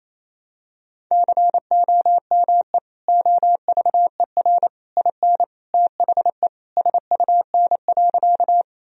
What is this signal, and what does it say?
Morse code, heard as a keyed tone, about 28 wpm, 700 Hz. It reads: COME OVER IN THE SUN.
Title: Sentences from Top 1000 Words - Review Part 1 - 28wpm